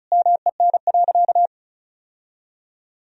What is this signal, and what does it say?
Morse code, heard as a keyed tone, about 36 wpm, 700 Hz. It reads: MEN.
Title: Sentences from Top 1000 Words - Review Part 1 - 35wpm